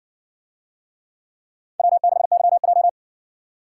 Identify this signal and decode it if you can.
Morse code, heard as a keyed tone, about 56 wpm, 700 Hz. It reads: W6YJ